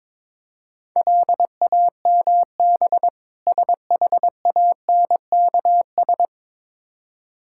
Morse code, heard as a keyed tone, about 22 wpm, 700 Hz. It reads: LAMB SHANKS
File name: Difficult Words with Hints - 22wpm